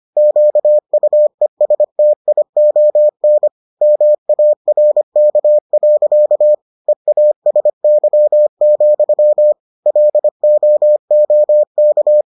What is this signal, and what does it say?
Morse code, heard as a keyed tone, about 25 wpm, 600 Hz. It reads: QUESTION MARK. EASY, LOOK